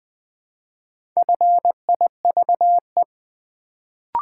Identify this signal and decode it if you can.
Morse code, heard as a keyed tone, about 20 wpm, 700 Hz. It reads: FIVE
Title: Build A Word - 1 Letter At A Time - 4-Letter Words - 20wpm